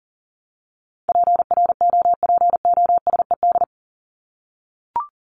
Morse code, heard as a keyed tone, about 40 wpm, 700 Hz. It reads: PROPOSED
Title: ED Suffix - Top 100 Words with Base Words 40wpm